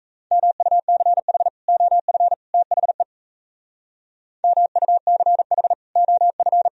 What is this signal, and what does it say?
Morse code, heard as a keyed tone, about 35 wpm, 700 Hz. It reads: MUCH OF THE MUCH OF
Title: Speed-Racing - Sets of 3 Words 28wpm